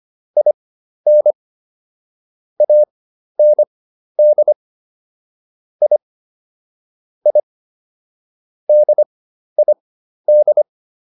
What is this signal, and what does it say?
Morse code, heard as a keyed tone, about 25 wpm, spaced as for 12 wpm, 600 Hz. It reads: IN AND I I DID